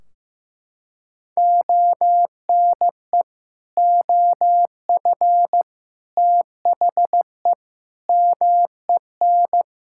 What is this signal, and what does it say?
Morse code, heard as a keyed tone, about 16 wpm, 700 Hz. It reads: ONE OF THE MEN
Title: Sentences from Top 1600 Words 15wpm